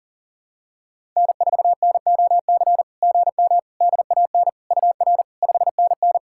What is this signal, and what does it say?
Morse code, heard as a keyed tone, about 43 wpm, 700 Hz. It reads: N4NOC GM DAN UR 5NN
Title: SOTA QSO Practice 40wpm